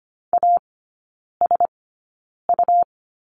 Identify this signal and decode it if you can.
Morse code, heard as a keyed tone, about 25 wpm, spaced as for 9 wpm, 700 Hz. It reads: ASU
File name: Lesson 17e - Letter u - Three Characters 25wpm